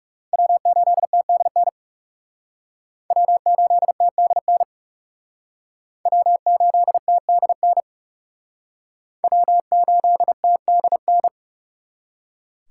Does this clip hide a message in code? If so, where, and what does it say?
Yes: Morse code from 0.3 to 11.3 s, heard as a keyed tone, about 39 wpm, 700 Hz. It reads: W8TDN W8TDN W8TDN W8TDN